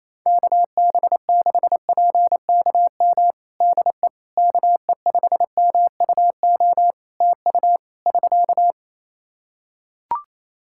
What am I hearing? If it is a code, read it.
Morse code, heard as a keyed tone, about 28 wpm, 700 Hz. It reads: KB6PKM DE KE5MUO TU <SK>
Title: Practice QSO - Exchange 4 - Ending - 28wpm